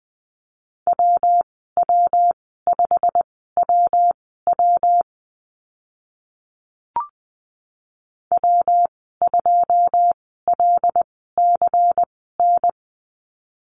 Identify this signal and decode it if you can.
Morse code, heard as a keyed tone, about 20 wpm, spaced as for 14 wpm, 700 Hz. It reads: WW5WW W2LCN
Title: 122 Lesson 20g - Number 3 - Call Signs 20wpm